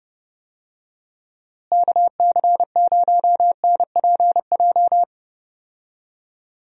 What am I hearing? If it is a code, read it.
Morse code, heard as a keyed tone, about 31 wpm, 700 Hz. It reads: KC0NPJ
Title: USA Call Signs - 6 Characters - Repeated 3x - 30wpm